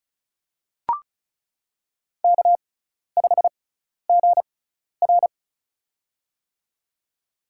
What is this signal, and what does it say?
Morse code, heard as a keyed tone, about 36 wpm, spaced as for 12 wpm, 700 Hz. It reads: K5GR
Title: USA Call Signs - 4 Characters - 6x Character Spacing - 35wpm